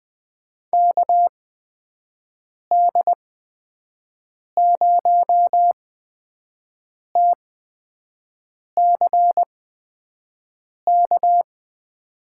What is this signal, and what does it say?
Morse code, heard as a keyed tone, about 20 wpm, spaced as for 5 wpm, 700 Hz. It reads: KD0TCK
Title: USA Call Signs - 8x Character Spacing - 20wpm